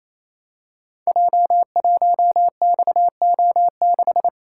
Morse code, heard as a keyed tone, about 28 wpm, 700 Hz. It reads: J1XO6
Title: Code Groups - 5 Letter:Numbers - 28wpm